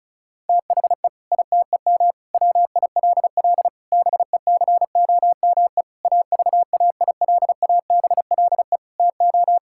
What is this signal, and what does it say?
Morse code, heard as a keyed tone, about 36 wpm, 700 Hz. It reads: THE ITEM WILL BECOME AVAILABLE TO
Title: Sentences from Top 1000 Words - Review Part 6 - 35wpm